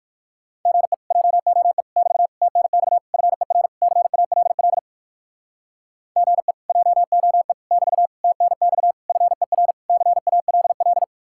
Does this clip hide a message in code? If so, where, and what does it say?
Yes: Morse code from 0.7 to 11.0 s, heard as a keyed tone, about 58 wpm, 700 Hz. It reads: GE JOE <BT> TNX FER CALL GE JOE <BT> TNX FER CALL